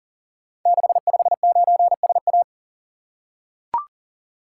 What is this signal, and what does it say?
Morse code, heard as a keyed tone, about 38 wpm, 690 Hz. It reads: 659SA
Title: Code Groups - 5 Letter:Numbers - 40wpm